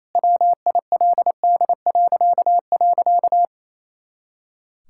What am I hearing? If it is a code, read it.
Morse code, heard as a keyed tone, about 28 wpm, 700 Hz. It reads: WILD..
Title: Pangram Challenge v2 - 28wpm